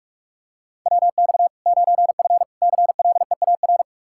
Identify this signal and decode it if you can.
Morse code, heard as a keyed tone, about 46 wpm, 700 Hz. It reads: WX 9F CLEAR
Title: K1USN SST QSO Practice 45wpm